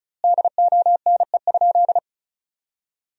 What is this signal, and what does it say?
Morse code, heard as a keyed tone, about 34 wpm, 700 Hz. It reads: DONE?
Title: 2x Word-Spacing - Sentences from Top 500 Words - 35wpm